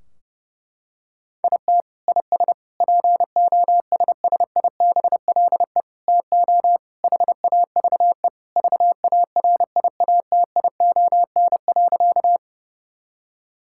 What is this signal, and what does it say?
Morse code, heard as a keyed tone, about 30 wpm, 700 Hz. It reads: IT IS POSSIBLE TO HAVE VARIATION.
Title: Sentences from Top 1600 Words 30wpm